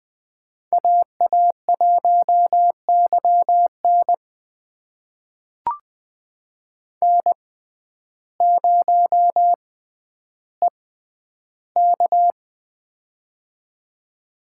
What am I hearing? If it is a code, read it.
Morse code, heard as a keyed tone, about 21 wpm, spaced as for 11 wpm, 700 Hz. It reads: AA1YN N0EK
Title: USA Call Signs - 6x Character Spacing - 20wpm